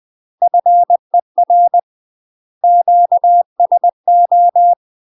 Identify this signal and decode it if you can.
Morse code, heard as a keyed tone, about 20 wpm, 700 Hz. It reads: FER QSO
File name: UK MASTER QSOs TRAINING / 20WPM / UK MASTER QSOs 0059